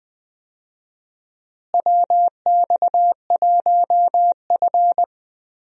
Morse code, heard as a keyed tone, about 20 wpm, 700 Hz. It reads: WX1F
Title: USA Call Signs - 4 Characters - Repeated 3x - 20wpm